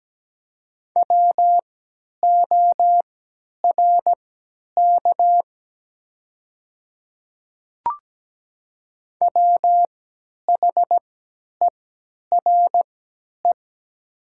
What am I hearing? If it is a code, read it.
Morse code, heard as a keyed tone, about 17 wpm, spaced as for 10 wpm, 700 Hz. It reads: WORK WHERE
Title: 3x Character Spacing - Top 300 Words - 17wpm